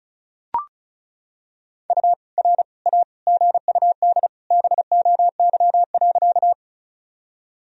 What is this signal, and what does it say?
Morse code, heard as a keyed tone, about 36 wpm, 700 Hz. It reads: U R A GUD BOY.
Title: Sentences from Top 200 Words and Abbreviations 35wpm